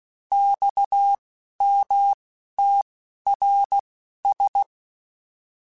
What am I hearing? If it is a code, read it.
Morse code, heard as a keyed tone, about 16 wpm, spaced as for 11 wpm, 800 Hz. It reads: XMTRS